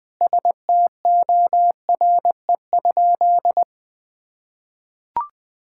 Morse code, heard as a keyed tone, about 20 wpm, 700 Hz. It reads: STORE?
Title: Sentences from Top 1000 Words - Review Part 5 - 20wpm